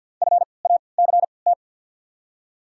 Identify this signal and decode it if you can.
Morse code, heard as a keyed tone, about 53 wpm, spaced as for 29 wpm, 690 Hz. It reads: FACT
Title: 3x Character Spacing - Top 500 Words - 50wpm